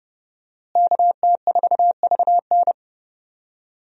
Morse code, heard as a keyed tone, about 30 wpm, 700 Hz. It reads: KT4VN
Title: USA Call Signs - 5 Characters - Repeated 2x - 30wpm